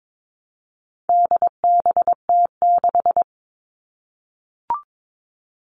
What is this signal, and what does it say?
Morse code, heard as a keyed tone, about 22 wpm, 700 Hz. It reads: DBT6
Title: All Cut Numbers - 4-Digit 22wpm